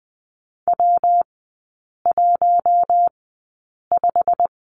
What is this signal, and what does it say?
Morse code, heard as a keyed tone, about 20 wpm, spaced as for 8 wpm, 700 Hz. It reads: W15
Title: 114 Lesson 19e - Letter w - Three Characters 20wpm